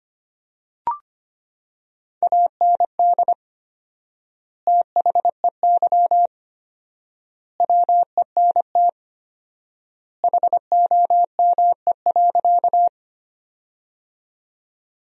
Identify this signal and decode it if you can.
Morse code, heard as a keyed tone, about 25 wpm, 700 Hz. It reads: AND THEY WENT HOME.
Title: 4x Word-Spacing - Sentences from Top 200 Words - 25wpm